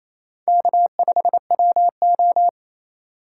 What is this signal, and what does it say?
Morse code, heard as a keyed tone, about 28 wpm, 700 Hz. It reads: K5WO
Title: USA Call Signs - 4 Characters - 2x Character Spacing - 28wpm